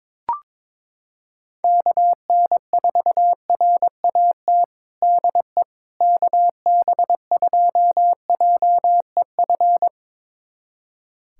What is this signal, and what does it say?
Morse code, heard as a keyed tone, about 22 wpm, 700 Hz. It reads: KN4RAT DE KB2JEF
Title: USA Callsign DE Callsign - 6 Characters - 22wpm